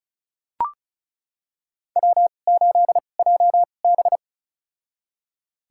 Morse code, heard as a keyed tone, about 34 wpm, spaced as for 25 wpm, 700 Hz. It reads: W8JB
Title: USA Call Signs - 2x Character Spacing - 35wpm